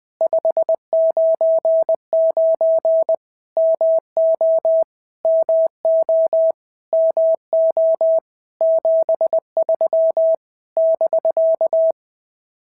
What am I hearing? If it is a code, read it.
Morse code, heard as a keyed tone, about 20 wpm, 650 Hz. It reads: 599 MO MO MO 73 <BK>
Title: POTA QSO Practice - 20wpm